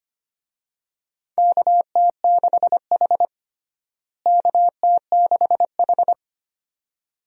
Morse code, heard as a keyed tone, about 25 wpm, 700 Hz. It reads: KT6H KT6H
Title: USA Call Signs - 4 Characters - Repeated 2x - 25wpm